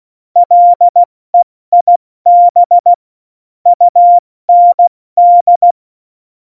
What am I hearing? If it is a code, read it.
Morse code, heard as a keyed tone, about 16 wpm, 700 Hz. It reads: LEIB UND
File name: Morsekurs / Witze / Witz 17. November 2020, 70 F80